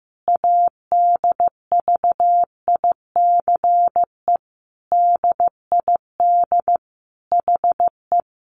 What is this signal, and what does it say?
Morse code, heard as a keyed tone, about 15 wpm, 700 Hz. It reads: ADVICE DID HE
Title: Sentences from Top 1100 Words 15wpm